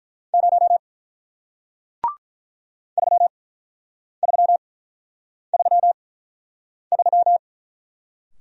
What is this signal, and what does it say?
Morse code, heard as a keyed tone, about 48 wpm, spaced as for 9 wpm, 700 Hz. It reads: 0 3333